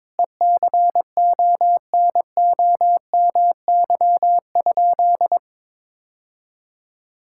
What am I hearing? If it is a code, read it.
Morse code, heard as a keyed tone, about 22 wpm, 700 Hz. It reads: ECONOMY?
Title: Sentences from Top 1000 Words - Review Part 6 - 22wpm